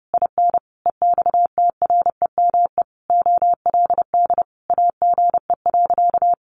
Morse code, heard as a keyed tone, about 30 wpm, 700 Hz. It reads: IN EXTREME OLD AGE.
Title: Sentences from Top 1300 Words 30wpm